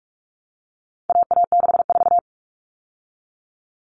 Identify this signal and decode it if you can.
Morse code, heard as a keyed tone, about 44 wpm, 700 Hz. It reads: AA64
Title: All Cut Numbers - 4-Digit 45wpm